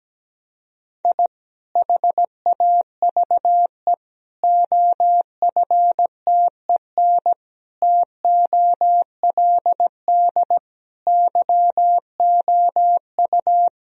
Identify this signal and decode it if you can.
Morse code, heard as a keyed tone, about 17 wpm, 700 Hz. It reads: I HAVE OFTEN TOLD YOU
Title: Sentences from Top 1000 Words - Review Part 1 - 17wpm